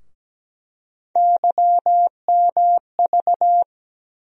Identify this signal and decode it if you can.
Morse code, heard as a keyed tone, about 17 wpm, 700 Hz. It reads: YMV
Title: USA Call Sign Postfixes - 3 Letters 17wpm